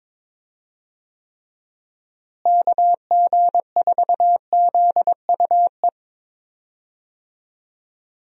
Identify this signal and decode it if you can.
Morse code, heard as a keyed tone, about 22 wpm, 700 Hz. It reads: KG4ZUE
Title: USA Call Signs - 6 Characters - Repeated 2x - 22wpm